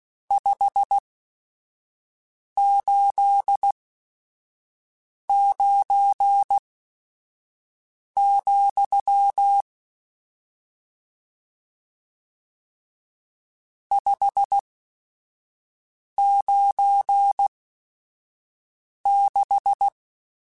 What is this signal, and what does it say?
Morse code, heard as a keyed tone, about 16 wpm, spaced as for 5 wpm, 800 Hz. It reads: 589, 596
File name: RSTs / RST-mp3s / Various RST @5